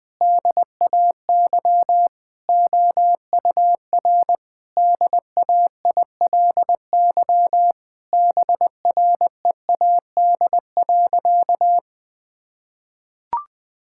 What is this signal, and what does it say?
Morse code, heard as a keyed tone, about 20 wpm, 700 Hz. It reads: DAY OUR DAILY BREAD.
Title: Sentences from Top 1600 Words 20wpm